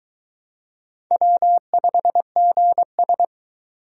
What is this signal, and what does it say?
Morse code, heard as a keyed tone, about 23 wpm, 700 Hz. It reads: W5GS